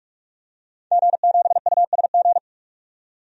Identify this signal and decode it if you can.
Morse code, heard as a keyed tone, about 46 wpm, 700 Hz. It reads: G7USG
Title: DXCC-100 Call Signs - 5 Characters - 45wpm